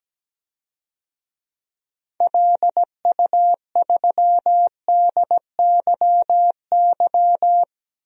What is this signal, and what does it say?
Morse code, heard as a keyed tone, about 17 wpm, 700 Hz. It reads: LU3DYY
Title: DXCC-100 Call Signs - 6 Characters - Repeated 2x - 17wpm